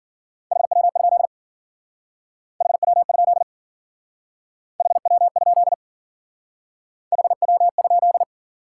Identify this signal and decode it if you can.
Morse code, heard as a keyed tone, about 48 wpm, 700 Hz. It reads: HW? HW? HW? HW?
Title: Speed-Racing Top 90 QSO Elements 40wpm